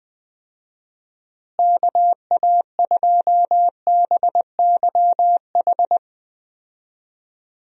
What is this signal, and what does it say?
Morse code, heard as a keyed tone, about 20 wpm, 700 Hz. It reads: KA2BYH
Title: USA Call Signs - 6 Characters - Repeated 2x - 20wpm